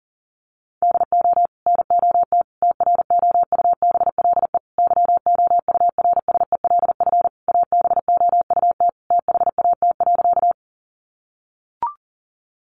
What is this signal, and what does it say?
Morse code, heard as a keyed tone, about 40 wpm, 700 Hz. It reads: DO NOT TROUBLE YOURSELF ABOUT THAT.
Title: Sentences from Top 1200 Words 40wpm